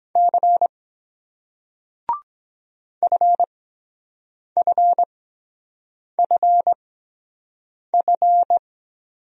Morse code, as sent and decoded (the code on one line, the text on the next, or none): -.-. / ..-. ..-. ..-. ..-.
C FFFF